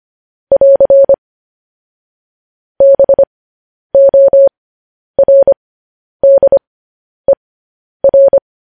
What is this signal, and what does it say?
Morse code, heard as a keyed tone, about 25 wpm, spaced as for 10 wpm, 550 Hz. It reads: <AR> BORDER
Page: News Headlines in Morse Code at 10 WPM